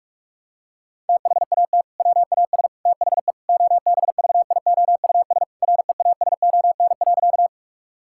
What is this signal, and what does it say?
Morse code, heard as a keyed tone, about 42 wpm, 700 Hz. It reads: THAT WAS THE OBVIOUS REASON.
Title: Sentences from Top 1600 Words 45wpm